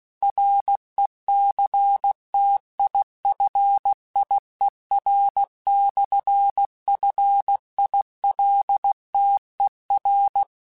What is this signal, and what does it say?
Morse code, heard as a keyed tone, about 16 wpm, 800 Hz. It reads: RECTIFIER/FILTER
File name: Technical Subjects / Technical Subjects-mp3s / Sw Pwr Sup @15